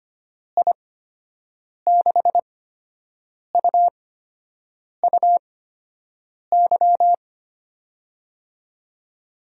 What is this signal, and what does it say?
Morse code, heard as a keyed tone, about 25 wpm, spaced as for 7 wpm, 700 Hz. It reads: I6UUY
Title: USA Call Signs - 6 Characters - 8x Character Spacing - 25wpm